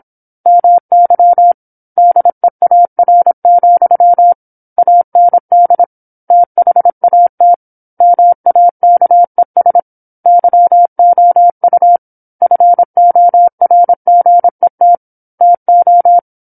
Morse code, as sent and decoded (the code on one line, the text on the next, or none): -- -.-- / -.. .- .-. --..-- / .- -. -.. / - .... .- - / -- .- -.- . ... / -.-- --- ..- / ..-. --- .-. --. - / - ---
MY DEAR, AND THAT MAKES YOU FORGET TO